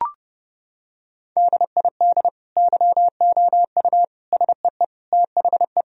Morse code, heard as a keyed tone, about 30 wpm, 700 Hz. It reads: DID YOU SEE THE
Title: Sentences from Top 1000 Words - Review Part 2 - 30wpm